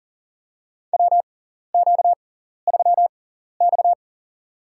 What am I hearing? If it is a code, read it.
Morse code, heard as a keyed tone, about 40 wpm, spaced as for 14 wpm, 700 Hz. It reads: WQ3X
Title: USA Call Signs - 4 Characters - 6x Character Spacing - 40wpm